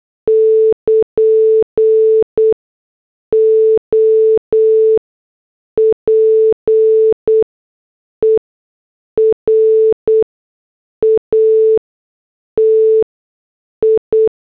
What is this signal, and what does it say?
Morse code, heard as a keyed tone, about 8 wpm, spaced as for 6 wpm, 440 Hz. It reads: <KN> OPERATI
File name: morse / headlines